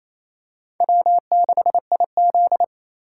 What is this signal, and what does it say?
Morse code, heard as a keyed tone, about 28 wpm, 700 Hz. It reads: W6IZ
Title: USA Call Signs - 3x Character Spacing - 28wpm